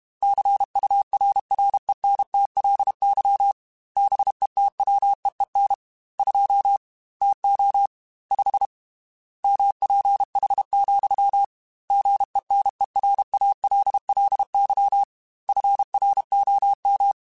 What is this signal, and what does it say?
Morse code, heard as a keyed tone, about 32 wpm, 800 Hz. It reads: CURRENTLY BETWEEN 2 TO 5 MPH, GENERALLY FROM
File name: Weather Sets / Weather Sets-mp3s / Weather 5 12 @30